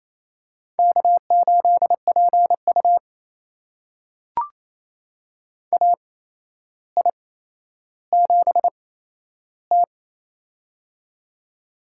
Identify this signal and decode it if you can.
Morse code, heard as a keyed tone, about 28 wpm, spaced as for 12 wpm, 700 Hz. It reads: K8PU AI7T
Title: USA Call Signs - 4 Characters - 8x Character Spacing - 28wpm